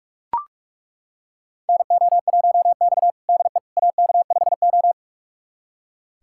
Morse code, heard as a keyed tone, about 41 wpm, 700 Hz. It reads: NO1X DE AK5O
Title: USA Callsign DE Callsign - 4 Characters - 45wpm